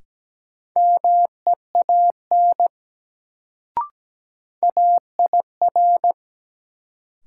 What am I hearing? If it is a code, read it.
Morse code, heard as a keyed tone, about 17 wpm, 700 Hz. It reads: MEAN AIR
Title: Sentences from Top 200 Words - New Word Drill - 17wpm